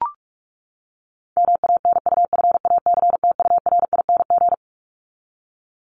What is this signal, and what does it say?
Morse code, heard as a keyed tone, about 44 wpm, 700 Hz. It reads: MANUFACTURING
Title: ING Suffix - Top 100 Words with Base Words 45wpm